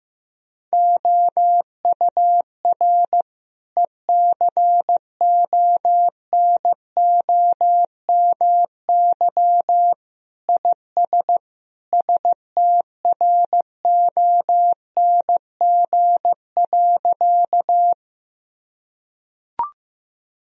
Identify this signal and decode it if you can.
Morse code, heard as a keyed tone, about 16 wpm, 700 Hz. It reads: OUR ECONOMY IS STRONG.
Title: Sentences from Top 1000 Words - Review Part 3 - 15wpm